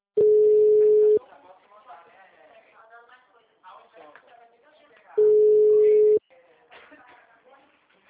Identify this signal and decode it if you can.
Ringback tone (Europe, 425 Hz), with speech.
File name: out- -2021- - - .42114